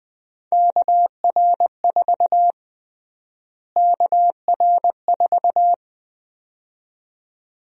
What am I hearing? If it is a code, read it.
Morse code, heard as a keyed tone, about 20 wpm, 700 Hz. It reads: KR4 KR4
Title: USA Call Sign Prefixes - Repeated 2x - 20wpm